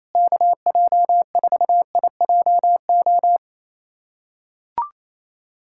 Morse code, heard as a keyed tone, about 28 wpm, 700 Hz. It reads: KJ4IJO
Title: USA Callsign DE Callsign - 6 Characters - 28wpm